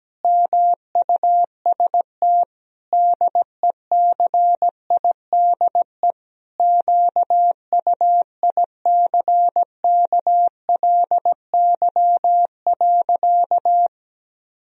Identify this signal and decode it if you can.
Morse code, heard as a keyed tone, about 17 wpm, 700 Hz. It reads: MUST DECIDE QUICKLY.
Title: Sentences from Top 1000 Words - Review Part 3 - 17wpm